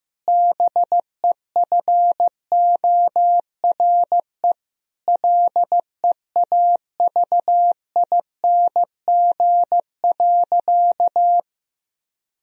Morse code, heard as a keyed tone, about 15 wpm, 700 Hz. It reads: BEFORE LEAVING.
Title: Sentences from Top 1000 Words - Review Part 4 - 15wpm